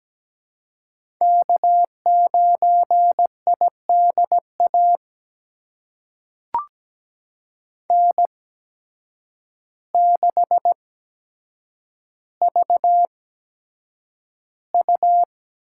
Morse code, heard as a keyed tone, about 17 wpm, spaced as for 8 wpm, 700 Hz. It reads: K9IDA N6VU
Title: USA Call Signs - 5 Characters - 8x Character Spacing - 17wpm